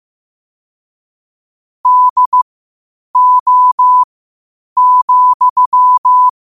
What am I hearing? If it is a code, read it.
Morse code, heard as a keyed tone, about 15 wpm, spaced as for 8 wpm, 1000 Hz. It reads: DO,
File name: qotdfiles / 8wpm